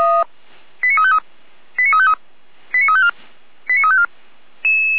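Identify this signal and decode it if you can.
Five-tone selective call (ZVEI), heard as quick runs of five tones, tones of about 70 ms. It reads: 89353 89353 89354 89354
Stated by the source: Five-tone ZVEI selective call received with an SDR.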